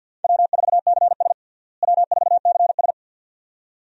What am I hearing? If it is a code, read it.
Morse code, heard as a keyed tone, about 50 wpm, 700 Hz. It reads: W4CS W4CS
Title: USA Call Signs - 4 Characters - Repeated 2x - 50wpm